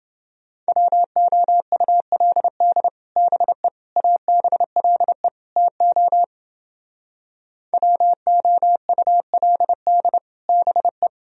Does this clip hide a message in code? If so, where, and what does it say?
Yes: Morse code from 0.7 to 11.1 s, heard as a keyed tone, about 30 wpm, 700 Hz. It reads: WOULD BE ABLE TO WOULD BE